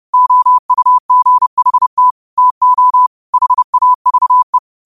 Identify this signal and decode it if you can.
Morse code, heard as a keyed tone, about 30 wpm, 1000 Hz. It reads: OUGHT TO HAVE
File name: qotdfiles / 30wpm